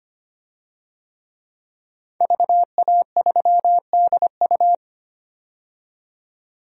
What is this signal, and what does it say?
Morse code, heard as a keyed tone, about 25 wpm, 700 Hz. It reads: VA3DU
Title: DXCC-100 Call Signs - 5 Characters - 25wpm